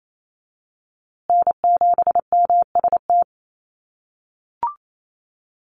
Morse code, heard as a keyed tone, about 28 wpm, 700 Hz. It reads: N7MST
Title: USA Call Signs Repeated 2x 28wpm